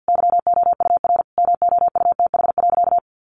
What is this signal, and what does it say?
Morse code, heard as a keyed tone, about 50 wpm, 700 Hz. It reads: YOUR MOUTH.
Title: Sentences from Top 1500 Words 50wpm